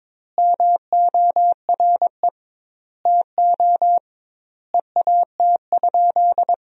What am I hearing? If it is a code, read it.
Morse code, heard as a keyed tone, about 22 wpm, 700 Hz. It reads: MORE TO EAT?
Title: 2x Word-Spacing - Sentences from Top 300 Words - 22wpm